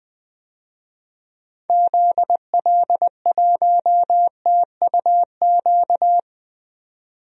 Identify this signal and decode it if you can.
Morse code, heard as a keyed tone, about 20 wpm, 700 Hz. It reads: ZL1TUQ
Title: DXCC-100 Call Signs - 6 Characters - 20wpm